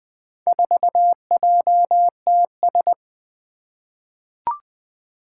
Morse code, heard as a keyed tone, about 20 wpm, 700 Hz. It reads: 4JTS
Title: Four Challenging Letter-Number Combinations 20wpm